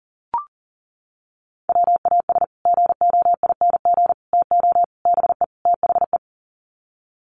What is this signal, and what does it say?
Morse code, heard as a keyed tone, about 40 wpm, 700 Hz. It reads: WAS GOING TO BE THE
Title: Sets of 5 Words Encore2 40wpm